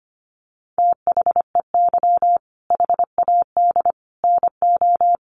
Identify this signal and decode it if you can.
Morse code, heard as a keyed tone, about 25 wpm, 700 Hz. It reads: THEY HAD NO
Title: Sentences from Top 1500 Words 25wpm